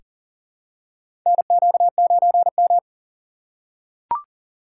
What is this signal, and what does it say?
Morse code, heard as a keyed tone, about 40 wpm, 700 Hz. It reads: NQ9M
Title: USA Call Signs - 4 Characters - 6x Character Spacing - 40wpm